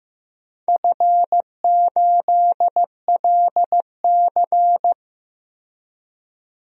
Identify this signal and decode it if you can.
Morse code, heard as a keyed tone, about 15 wpm, 700 Hz. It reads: F8LC
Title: DXCC-100 Call Signs - 4 Characters - 15wpm